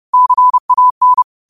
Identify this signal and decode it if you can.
Morse code, heard as a keyed tone, about 30 wpm, 1000 Hz. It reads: CAN
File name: qotdfiles / 30wpm